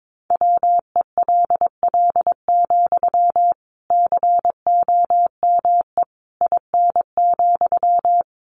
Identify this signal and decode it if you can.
Morse code, heard as a keyed tone, about 22 wpm, 700 Hz. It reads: WELL, COME IN,
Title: Sentences from Top 1000 Words 22wpm